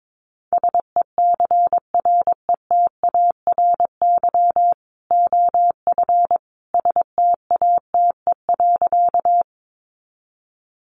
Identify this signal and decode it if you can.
Morse code, heard as a keyed tone, about 22 wpm, 700 Hz. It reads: SECRETARY OF STATE.
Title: Sentences from Top 1300 Words 22wpm